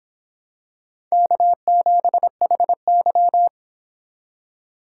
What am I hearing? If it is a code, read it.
Morse code, heard as a keyed tone, about 25 wpm, 700 Hz. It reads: K7HY